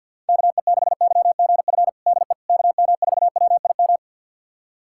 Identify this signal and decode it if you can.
Morse code, heard as a keyed tone, about 51 wpm, 700 Hz. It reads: KE6YGF DE KM4WIM
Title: USA Callsign DE Callsign - 6 Characters - 50wpm